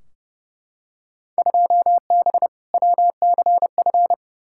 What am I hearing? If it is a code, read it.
Morse code, heard as a keyed tone, about 30 wpm, 700 Hz. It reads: 2B WCF
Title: Field Day Exchanges Encore - 30wpm